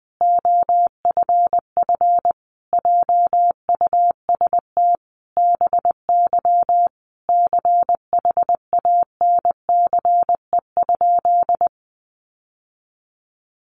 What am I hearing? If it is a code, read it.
Morse code, heard as a keyed tone, about 20 wpm, 700 Hz. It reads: OFF JUST BY CHANCE?
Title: Sentences from Top 800 Words 20wpm